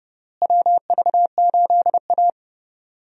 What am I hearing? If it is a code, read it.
Morse code, heard as a keyed tone, about 29 wpm, 700 Hz. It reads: WV8A
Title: USA Call Signs - 4 Characters - Repeated 3x - 30wpm